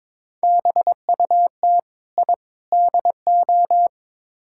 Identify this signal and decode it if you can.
Morse code, heard as a keyed tone, about 21 wpm, 700 Hz. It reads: BUT I DO
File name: practice / 22wpm / Build A Phrase - 5-Word Phrases - 22wpm